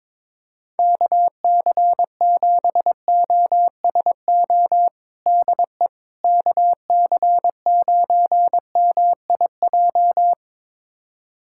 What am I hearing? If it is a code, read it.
Morse code, heard as a keyed tone, about 22 wpm, 700 Hz. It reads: KC7OSO DE KC9MIJ
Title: USA Callsign DE Callsign - 6 Characters - 22wpm